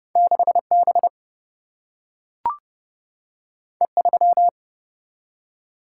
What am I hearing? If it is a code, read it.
Morse code, heard as a keyed tone, about 30 wpm, 700 Hz. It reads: 6B E3
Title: Single-Letter Single-Number 30wpm